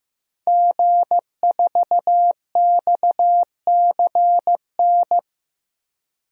Morse code, heard as a keyed tone, about 15 wpm, 700 Hz. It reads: G4XCN
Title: DXCC-100 Call Signs - 5 Characters - Repeated 2x - 15wpm